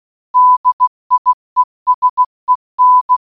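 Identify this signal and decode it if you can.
Morse code, heard as a keyed tone, about 16 wpm, 1000 Hz. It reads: DIESEN
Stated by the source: Morse audio for a geocache puzzle, GCB8MYB Morse-Bonus Reloaded, in Baden-Württemberg, Germany, created by Wolli.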